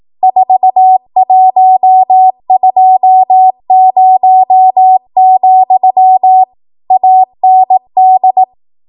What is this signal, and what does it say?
Morse code, heard as a keyed tone, about 18 wpm, 750 Hz. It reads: 4120, AND
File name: Morse / 18WPM